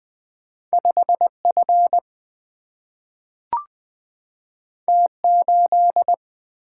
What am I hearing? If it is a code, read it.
Morse code, heard as a keyed tone, about 20 wpm, 700 Hz. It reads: 5F T8
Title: Single-Letter Single-Number 20wpm